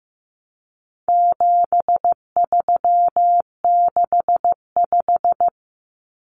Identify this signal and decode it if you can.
Morse code, heard as a keyed tone, about 15 wpm, 700 Hz. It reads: 7365
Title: Common Cut Numbers - 4-Digit 15wpm